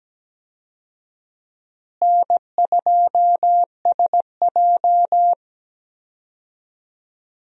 Morse code, heard as a keyed tone, about 17 wpm, 700 Hz. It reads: N2SJ
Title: USA Call Signs - 4 Characters - Repeated 2x - 17wpm